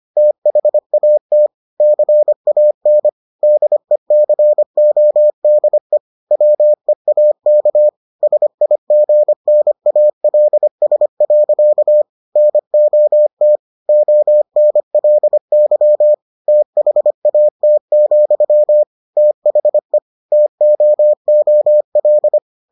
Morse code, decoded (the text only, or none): THAT CAN DECODE WEAK SIGNALS. NOT ONLY THAT, THE TOOL